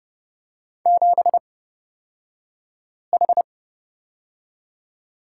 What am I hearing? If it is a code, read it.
Morse code, heard as a keyed tone, about 30 wpm, spaced as for 5 wpm, 700 Hz. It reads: 7H
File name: Lesson 38d - Letter q - Two Characters 30wpm